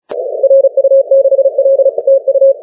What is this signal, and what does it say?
Morse code, heard as a keyed tone, about 36 wpm, 540 Hz. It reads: RU6L TU
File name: wp / tu2t 10m cw 28 oct 2011